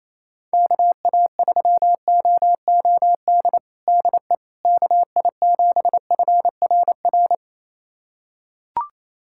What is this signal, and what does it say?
Morse code, heard as a keyed tone, about 28 wpm, 700 Hz. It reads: KA3OOD DE KI7FRR
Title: USA Callsign DE Callsign 28wpm